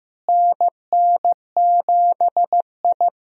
Morse code, heard as a keyed tone, about 15 wpm, 700 Hz. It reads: NN7I
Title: USA Call Signs - 4 Characters - Repeated 2x - 15wpm